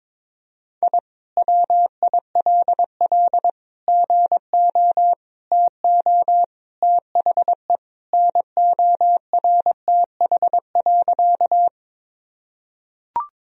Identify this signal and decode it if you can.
Morse code, heard as a keyed tone, about 22 wpm, 700 Hz. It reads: I WILL GO TO THE NORTH.
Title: Sentences from Top 1000 Words - Review Part 1 - 22wpm